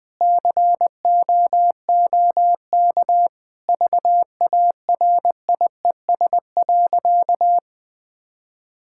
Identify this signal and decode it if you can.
Morse code, heard as a keyed tone, about 19 wpm, 700 Hz. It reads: COOK VARIES.
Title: Sentences with 7-Word Phrases - 20wpm